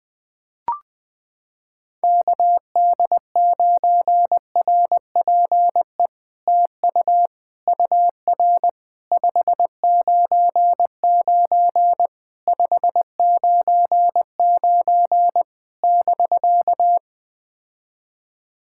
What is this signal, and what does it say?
Morse code, heard as a keyed tone, about 20 wpm, 700 Hz. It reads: KD9RPE TU UR 599 599 <BK>
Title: POTA QSO Practice - 20wpm